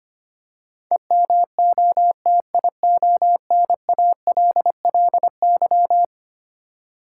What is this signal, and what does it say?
Morse code, heard as a keyed tone, about 25 wpm, 700 Hz. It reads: EMOTIONALLY
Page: morse-code-ninja-practice-25wpm